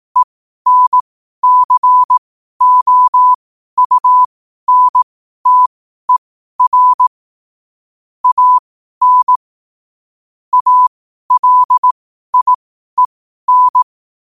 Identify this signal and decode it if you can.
Morse code, heard as a keyed tone, about 18 wpm, spaced as for 12 wpm, 1000 Hz. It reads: ENCOUNTER AN ALIEN
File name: qotdfiles / 12wpm